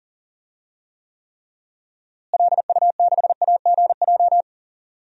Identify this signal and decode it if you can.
Morse code, heard as a keyed tone, about 40 wpm, 700 Hz. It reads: LU6AGJ
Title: DXCC-100 Call Signs - 6 Characters - Repeated 3x - 40wpm